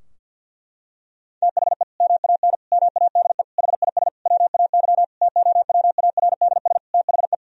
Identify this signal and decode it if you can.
Morse code, heard as a keyed tone, about 51 wpm, 700 Hz. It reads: THE MAN MADE HIS WAY TOWARDS THE
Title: Sentences from Top 1000 Words - Review Part 2 - 50wpm